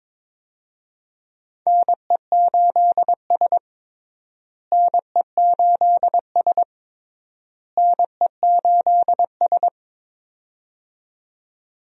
Morse code, heard as a keyed tone, about 21 wpm, 700 Hz. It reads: NE8S NE8S NE8S
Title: USA Call Signs - 4 Characters - Repeated 3x - 22wpm